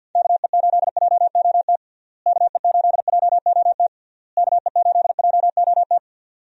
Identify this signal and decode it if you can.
Morse code, heard as a keyed tone, about 52 wpm, 700 Hz. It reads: KE8JOT KE8JOT KE8JOT
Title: USA Call Signs - 6 Characters - Repeated 3x - 50wpm